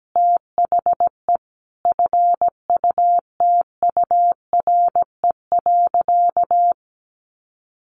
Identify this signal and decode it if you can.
Morse code, heard as a keyed tone, about 17 wpm, 700 Hz. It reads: THE FUTURE.
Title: Sentences from Top 1500 Words 17wpm